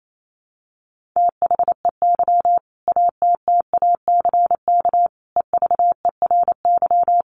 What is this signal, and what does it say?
Morse code, heard as a keyed tone, about 28 wpm, 700 Hz. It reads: THEY ATTACK EVERY
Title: Sentences from Top 1500 Words 28wpm